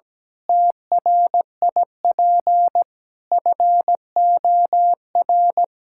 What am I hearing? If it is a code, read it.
Morse code, heard as a keyed tone, about 17 wpm, 700 Hz. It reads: TRIP FOR
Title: Sentences from Top 1600 Words 17wpm